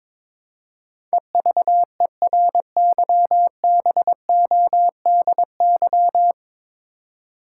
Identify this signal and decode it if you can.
Morse code, heard as a keyed tone, about 22 wpm, 700 Hz. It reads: EVERYBODY
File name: practice / 22wpm / 9 Letter Words - First 3 Letters Given - 22wpm